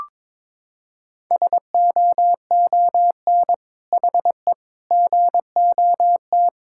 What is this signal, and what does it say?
Morse code, heard as a keyed tone, about 22 wpm, 700 Hz. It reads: SOON HE GOT
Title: Sentences from Top 1600 Words 22wpm